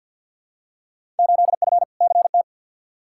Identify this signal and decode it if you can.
Morse code, heard as a keyed tone, about 47 wpm, 700 Hz. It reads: 8F CT
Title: Field Day Exchanges Encore - 50wpm